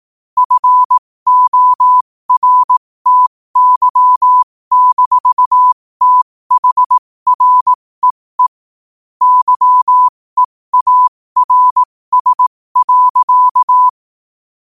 Morse code, heard as a keyed tone, about 18 wpm, spaced as for 15 wpm, 1000 Hz. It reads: FORTY-THREE YEARS.
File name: qotdfiles / 15wpm